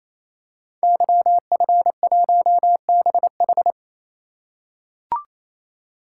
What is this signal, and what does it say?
Morse code, heard as a keyed tone, about 27 wpm, 700 Hz. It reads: YF1BH
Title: Code Groups - 5 Letter:Numbers - 28wpm